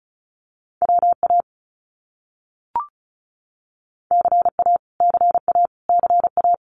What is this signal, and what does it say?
Morse code, heard as a keyed tone, about 34 wpm, 700 Hz. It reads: WA CA CA CA
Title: US States Repeated 3x 35wpm